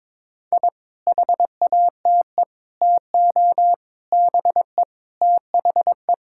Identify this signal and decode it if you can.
Morse code, heard as a keyed tone, about 22 wpm, 700 Hz. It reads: I HATE TO BE THE
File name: Sets of 9 Words - 22wpm